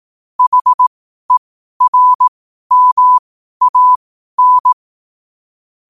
Morse code, heard as a keyed tone, about 18 wpm, spaced as for 12 wpm, 1000 Hz. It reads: HERMAN
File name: qotdfiles / 12wpm